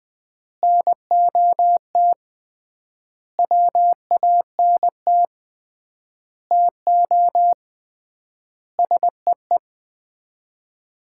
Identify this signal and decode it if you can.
Morse code, heard as a keyed tone, about 20 wpm, 700 Hz. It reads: NOT WANT TO SEE
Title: 3x Word-Spacing - Sets of 5 Words - 20wpm